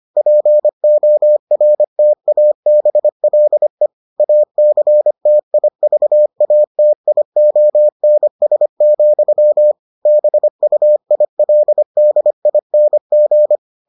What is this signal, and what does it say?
Morse code, heard as a keyed tone, about 25 wpm, 600 Hz. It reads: PORTABLE ACTIVATIONS, BUILDING